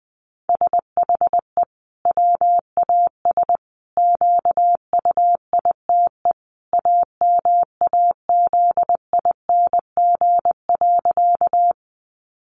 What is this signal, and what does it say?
Morse code, heard as a keyed tone, about 20 wpm, 700 Hz. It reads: SHE WAS QUITE AMAZING.
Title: Sentences from Top 1500 Words 20wpm